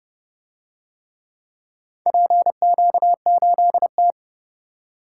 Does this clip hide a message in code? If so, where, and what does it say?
Yes: Morse code from 2.1 to 4.1 s, heard as a keyed tone, about 29 wpm, 700 Hz. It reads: PQ8T